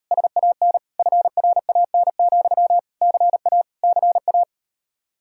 Morse code, heard as a keyed tone, about 38 wpm, 700 Hz. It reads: SAN FRAN, CA CA